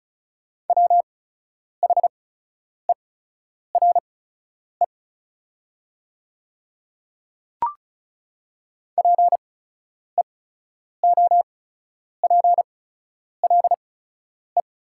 Morse code, heard as a keyed tone, about 36 wpm, spaced as for 10 wpm, 700 Hz. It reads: WHERE PEOPLE
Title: 8x Character Spacing - Top 200 Words - 35wpm